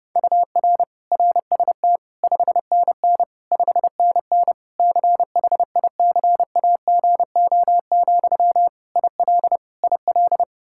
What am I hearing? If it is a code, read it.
Morse code, heard as a keyed tone, about 30 wpm, 700 Hz. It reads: UR RST 5NN 5NN CHICAGO, IL IL
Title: K1USN SST QSO Practice 30wpm